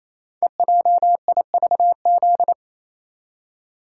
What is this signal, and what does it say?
Morse code, heard as a keyed tone, about 27 wpm, 690 Hz. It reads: EJIVZ
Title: Code Groups - 5 Letters - 28wpm